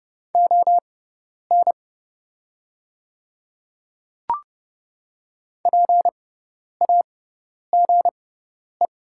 Morse code, heard as a keyed tone, about 30 wpm, spaced as for 10 wpm, 700 Hz. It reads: ON PAGE
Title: 6x Character Spacing - Top 300 Words - 30wpm